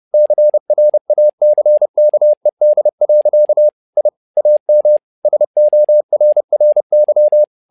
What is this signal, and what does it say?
Morse code, heard as a keyed tone, about 30 wpm, 600 Hz. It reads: CRACKED. I AM SORRY
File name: learning center / tower trouble / tower trouble 30wpm